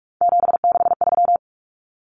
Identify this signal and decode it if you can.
Morse code, heard as a keyed tone, about 44 wpm, 700 Hz. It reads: 763
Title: Numbers - 3-Digit 45wpm